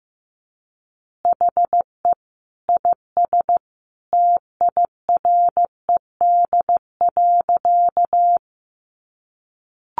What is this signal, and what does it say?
Morse code, heard as a keyed tone, about 15 wpm, 700 Hz. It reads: HE IS TIRED.
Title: Additional Easy Phrases 15wpm